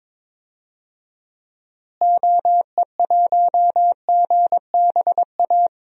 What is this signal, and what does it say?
Morse code, heard as a keyed tone, about 22 wpm, 700 Hz. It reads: OE1GBA
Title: DXCC-100 Call Signs - 6 Characters - Repeated 3x - 22wpm